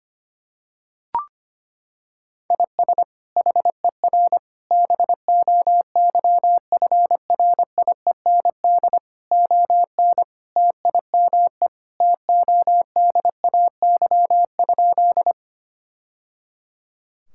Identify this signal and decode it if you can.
Morse code, heard as a keyed tone, about 25 wpm, 700 Hz. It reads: IS HER BOYFRIEND ON TIME TODAY?
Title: Sentences from Top 1000 Words - Review Part 5 - 25wpm